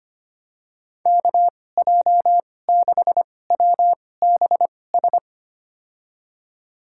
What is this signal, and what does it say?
Morse code, heard as a keyed tone, about 25 wpm, spaced as for 18 wpm, 700 Hz. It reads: KJ6WBS
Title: USA Call Signs - 6 Characters - 2x Character Spacing - 25wpm